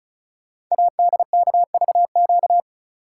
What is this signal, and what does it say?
Morse code, heard as a keyed tone, about 36 wpm, 700 Hz. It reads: ADKVQ
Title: Code Groups - 5 Letters - 35wpm